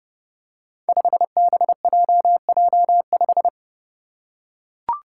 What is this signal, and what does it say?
Morse code, heard as a keyed tone, about 30 wpm, 700 Hz. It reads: 5BJJ5
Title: Five Challenging Letter-Number Combinations 30wpm